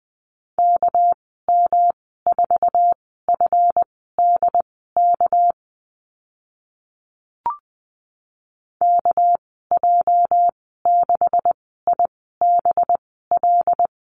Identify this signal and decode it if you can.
Morse code, heard as a keyed tone, about 20 wpm, spaced as for 14 wpm, 700 Hz. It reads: KM4FDK KJ6IBL
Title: 219 Lesson 35g - Number 0 - Call Signs 20wpm